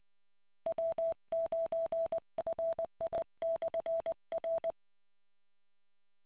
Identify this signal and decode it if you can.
Morse code, heard as a keyed tone, about 20 wpm, 660 Hz. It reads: W9RI/R